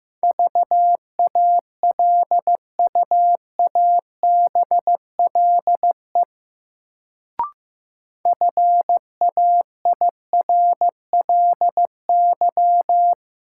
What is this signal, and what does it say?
Morse code, heard as a keyed tone, about 15 wpm, 700 Hz. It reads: VALUABLE FAIRLY
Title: Sentences from Top 1100 Words - New Word Drill - 15wpm